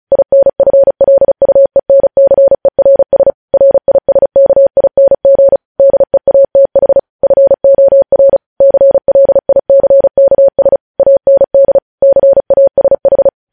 Morse code, heard as a keyed tone, about 36 wpm, 550 Hz. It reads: INFLUENCERS RISKING DEATH FOR CLICKS AND CASH